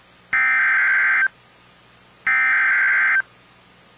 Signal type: Emergency Alert System (SAME) tones